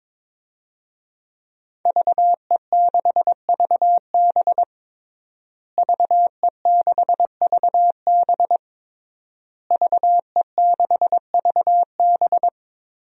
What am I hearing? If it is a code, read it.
Morse code, heard as a keyed tone, about 22 wpm, 700 Hz. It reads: VE6VB VE6VB VE6VB
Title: DXCC-100 Call Signs - 5 Characters - Repeated 3x - 22wpm